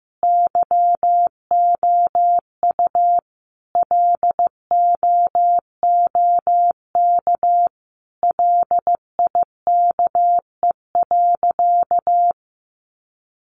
Morse code, heard as a keyed tone, about 15 wpm, 700 Hz. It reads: YOU LOOK LIKE.
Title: Sentences from Top 1300 Words 15wpm